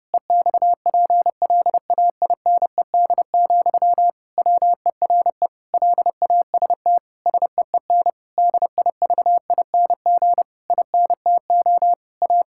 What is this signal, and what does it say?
Morse code, heard as a keyed tone, about 32 wpm, 700 Hz. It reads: EXPLAINED, WERE LAST SEEN DIVING INTO A
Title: Pangram Challenge v2 - 30wpm